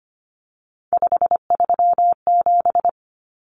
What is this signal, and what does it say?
Morse code, heard as a keyed tone, about 25 wpm, 700 Hz. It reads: 537
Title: Numbers - 3-Digit 25wpm